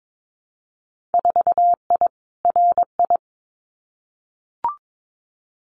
Morse code, heard as a keyed tone, about 22 wpm, 700 Hz. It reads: 4I RI
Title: Winter Field Day Exchanges Encore 22wpm